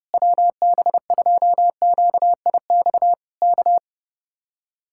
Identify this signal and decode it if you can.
Morse code, heard as a keyed tone, about 29 wpm, 700 Hz. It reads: WB2QIX K
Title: Speed-Racing Common QSO Phrases Encore - 30wpm